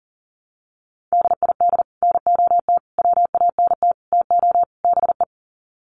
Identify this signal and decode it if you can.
Morse code, heard as a keyed tone, about 40 wpm, 700 Hz. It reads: DID NOT WANT TO BE
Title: Sets of 5 Words Encore 40wpm